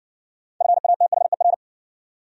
Morse code, heard as a keyed tone, about 58 wpm, 700 Hz. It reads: FATHER